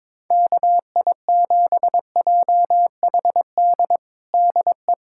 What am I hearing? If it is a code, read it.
Morse code, heard as a keyed tone, about 22 wpm, 700 Hz. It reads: KI7JHD DE
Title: USA Callsign DE Callsign 22wpm